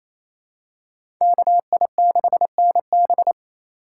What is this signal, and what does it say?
Morse code, heard as a keyed tone, about 28 wpm, 700 Hz. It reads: KI6NB
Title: USA Call Signs - 5 Characters - 2x Character Spacing - 28wpm